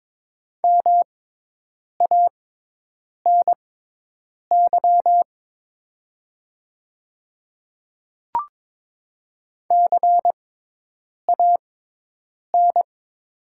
Morse code, heard as a keyed tone, about 22 wpm, spaced as for 8 wpm, 700 Hz. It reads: MANY CAN
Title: 6x Character Spacing - Top 500 Words - 22wpm